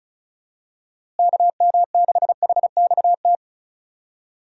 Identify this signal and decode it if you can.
Morse code, heard as a keyed tone, about 34 wpm, 700 Hz. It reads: KM6HXT
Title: USA Call Signs - 6 Characters - Repeated 2x - 35wpm